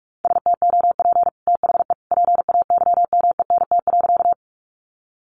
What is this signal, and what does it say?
Morse code, heard as a keyed tone, about 44 wpm, 700 Hz. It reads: STOP THE PAYMENT.
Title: Sentences from Top 1200 Words 45wpm